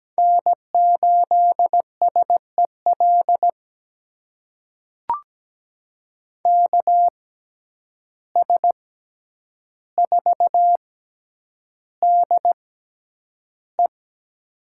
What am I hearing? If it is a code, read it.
Morse code, heard as a keyed tone, about 17 wpm, spaced as for 9 wpm, 700 Hz. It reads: N8SEL KS4DE
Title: USA Call Signs - 5 Characters - 6x Character Spacing - 17wpm